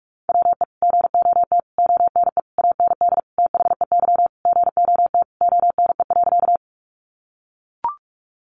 Morse code, heard as a keyed tone, about 44 wpm, 700 Hz. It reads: WE GOT ONE AND THEY GOT ONE.
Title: Sentences from Top 300 Words 45wpm